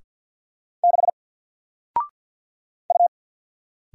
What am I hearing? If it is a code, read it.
Morse code, heard as a keyed tone, about 50 wpm, 700 Hz. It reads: 6 U
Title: Single Character (Letter:Number:Punctuation:Prosign) 50wpm